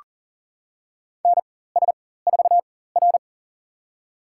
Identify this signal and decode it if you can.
Morse code, heard as a keyed tone, about 40 wpm, spaced as for 19 wpm, 700 Hz. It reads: NS4R
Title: USA Call Signs - 4x Character Spacing - 40wpm